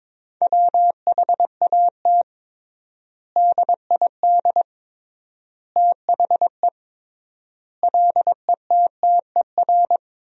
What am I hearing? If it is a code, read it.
Morse code, heard as a keyed tone, about 22 wpm, 700 Hz. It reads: WHAT DID THE LETTER
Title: 3x Word-Spacing - Sentences from Top 500 Words - 22wpm